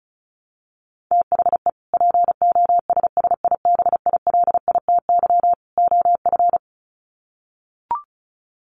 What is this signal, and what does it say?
Morse code, heard as a keyed tone, about 34 wpm, 700 Hz. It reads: THE POSSIBILITY OF
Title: Sets of 3 Words 35wpm